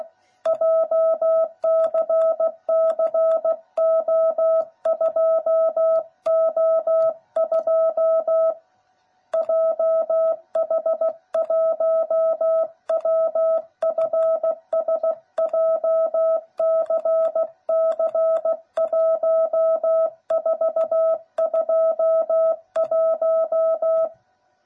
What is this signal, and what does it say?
Morse code, heard as a keyed tone, about 16 wpm, 660 Hz. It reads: JCCO2O2 JH1WFS JCC1421
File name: jh8cbh / 2021cw / cw09b